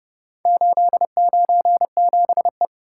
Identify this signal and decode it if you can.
Morse code, heard as a keyed tone, about 29 wpm, 700 Hz. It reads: 897E